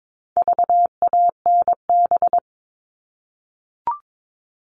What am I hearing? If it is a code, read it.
Morse code, heard as a keyed tone, about 22 wpm, 700 Hz. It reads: VANB
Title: All Cut Numbers - 4-Digit 22wpm